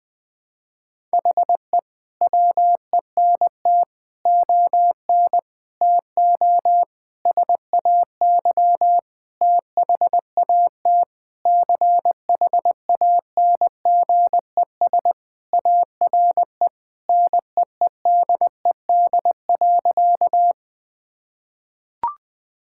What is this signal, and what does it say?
Morse code, heard as a keyed tone, about 20 wpm, 700 Hz. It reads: HE WENT ON TO SAY THAT CHANGES ARE NEEDED.
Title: Sentences with 6-Word Phrases - 20wpm